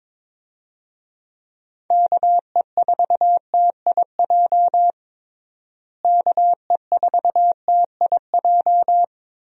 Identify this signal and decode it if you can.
Morse code, heard as a keyed tone, about 22 wpm, 700 Hz. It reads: KE4TIJ KE4TIJ
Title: USA Call Signs - 6 Characters - Repeated 2x - 22wpm